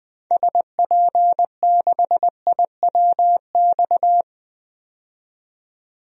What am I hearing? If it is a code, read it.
Morse code, heard as a keyed tone, about 21 wpm, 700 Hz. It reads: SP6IWX
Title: DXCC-100 Call Signs - 6 Characters - 20wpm